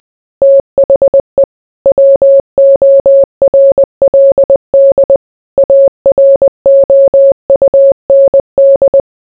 Morse code, heard as a keyed tone, about 20 wpm, 550 Hz. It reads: THE WORLD AROUND